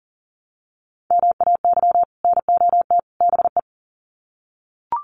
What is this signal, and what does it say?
Morse code, heard as a keyed tone, about 40 wpm, 700 Hz. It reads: MAY NOT BE
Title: Sets of 3 Words 40wpm